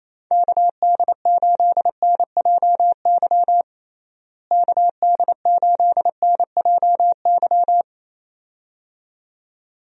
Morse code, heard as a keyed tone, about 28 wpm, 700 Hz. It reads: KD8NJY KD8NJY
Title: USA Call Signs - 6 Characters - Repeated 2x - 28wpm